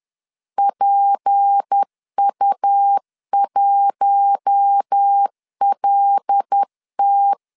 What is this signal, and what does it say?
Morse code, heard as a keyed tone, about 10 wpm, 800 Hz. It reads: PU1LT
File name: LEZIONI / 7 GF9Z / 7 GF9Z 10wpm 2611